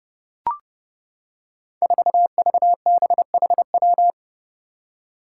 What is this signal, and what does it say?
Morse code, heard as a keyed tone, about 32 wpm, 700 Hz. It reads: 4VBHW
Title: Five Challenging Letter-Number Combinations 30wpm